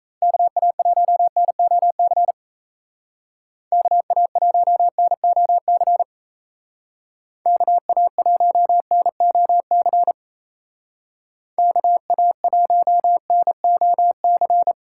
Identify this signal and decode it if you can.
Morse code, heard as a keyed tone, about 37 wpm, 700 Hz. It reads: KA1NOC KA1NOC KA1NOC KA1NOC